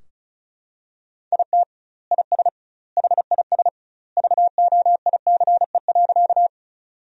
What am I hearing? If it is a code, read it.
Morse code, heard as a keyed tone, about 34 wpm, 700 Hz. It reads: IT IS HIS VOICE.
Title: 2x Word-Spacing - Sentences from Top 500 Words - 35wpm